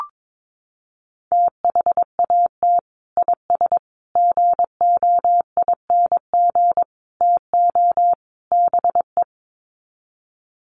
Morse code, heard as a keyed tone, about 22 wpm, 700 Hz. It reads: THAT IS GOING TO BE
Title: Sets of 5 Words Encore2 22wpm